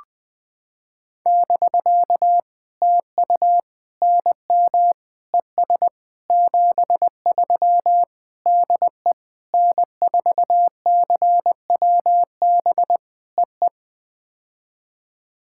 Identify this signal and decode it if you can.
Morse code, heard as a keyed tone, about 19 wpm, 700 Hz. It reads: <BK> TU NM ES 73 DE N4CWB EE
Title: POTA QSO Practice - 20wpm